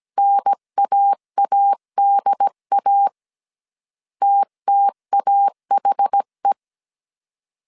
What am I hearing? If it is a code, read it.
Morse code, heard as a keyed tone, about 17 wpm, 800 Hz. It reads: NAADA TTAHE
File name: CW / DET / 1 3 1500C S A 16wpm 2141